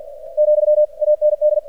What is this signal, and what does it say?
Morse code, heard as a keyed tone, about 48 wpm, 600 Hz. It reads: , AND
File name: archived / CRSnet